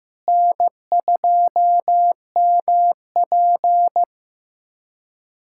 Morse code, heard as a keyed tone, about 15 wpm, 700 Hz. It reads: N2MP
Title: USA Callsign DE Callsign 15wpm